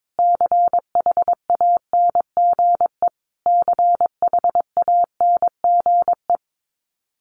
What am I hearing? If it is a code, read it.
Morse code, heard as a keyed tone, about 22 wpm, 700 Hz. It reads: CHANGE CHANGE
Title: Top 500 Words Repeated 22wpm